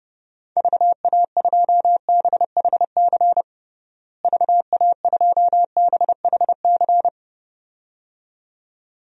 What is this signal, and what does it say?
Morse code, heard as a keyed tone, about 32 wpm, 700 Hz. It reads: VA2BHC VA2BHC
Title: DXCC-100 Call Signs - 6 Characters - Repeated 2x - 30wpm